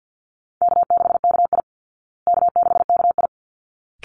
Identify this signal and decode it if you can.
Morse code, heard as a keyed tone, about 48 wpm, 700 Hz. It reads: K6KI K6KI
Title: Easy Call Signs Repeated 2x - 50wpm